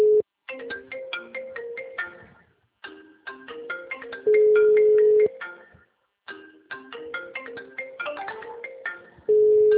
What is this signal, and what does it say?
Ringback tone (Europe, 425 Hz).
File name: out- -2015- - - .17354